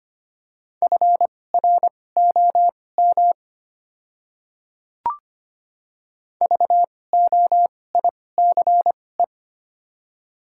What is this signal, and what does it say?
Morse code, heard as a keyed tone, about 25 wpm, spaced as for 18 wpm, 700 Hz. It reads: FROM VOICE
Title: 2x Character Spacing - Top 500 Words - 25wpm